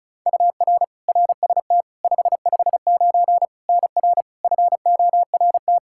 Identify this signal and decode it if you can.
Morse code, heard as a keyed tone, about 36 wpm, 700 Hz. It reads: UR RST 559 NR FORT
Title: Practice QSO - Exchange 1 - Beginning - 35wpm